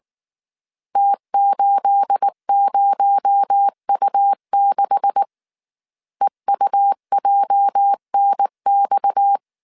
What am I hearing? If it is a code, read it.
Morse code, heard as a keyed tone, about 19 wpm, 800 Hz. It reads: T80U6 EUJNX
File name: CW / DET / 1 9 1500C S A 18wpm 2459